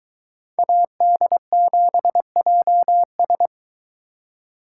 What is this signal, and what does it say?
Morse code, heard as a keyed tone, about 23 wpm, 700 Hz. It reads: AD7JS